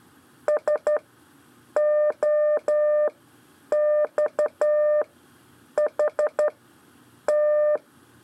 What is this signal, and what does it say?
Morse code, heard as a keyed tone, about 12 wpm, spaced as for 8 wpm, 600 Hz. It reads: SOXHT
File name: 4 LEZIONE CW 1